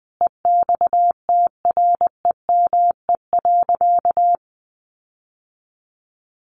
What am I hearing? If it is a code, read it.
Morse code, heard as a keyed tone, about 19 wpm, 700 Hz. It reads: EXTREME.
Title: Sentences from Top 1500 Words 20wpm